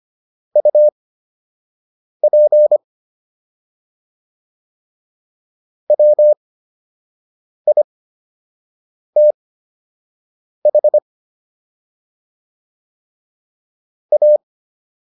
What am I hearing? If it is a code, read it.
Morse code, heard as a keyed tone, about 25 wpm, spaced as for 6 wpm, 600 Hz. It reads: UP WITH A